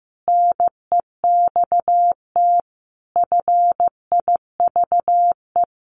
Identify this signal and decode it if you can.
Morse code, heard as a keyed tone, about 15 wpm, 700 Hz. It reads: NEXT FIVE
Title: Sets of 5 Words 15wpm